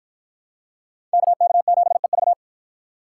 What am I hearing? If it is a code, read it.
Morse code, heard as a keyed tone, about 53 wpm, 700 Hz. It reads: KK7EV